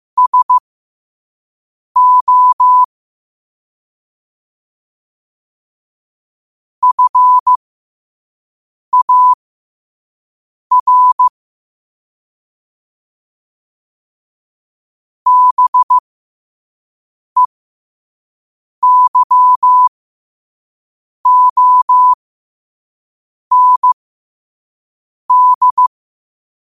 Morse code, heard as a keyed tone, about 15 wpm, spaced as for 5 wpm, 1000 Hz. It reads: SO FAR BEYOND
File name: qotdfiles / 5wpm